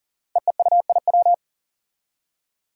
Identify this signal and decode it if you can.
Morse code, heard as a keyed tone, about 41 wpm, 700 Hz. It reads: EEUIW
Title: Five Challenging Letter-Number Combinations 40wpm